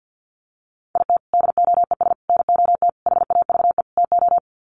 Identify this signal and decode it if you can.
Morse code, heard as a keyed tone, about 50 wpm, 700 Hz. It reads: IT DOES NOT HAVE TO
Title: Sets of 5 Words 50wpm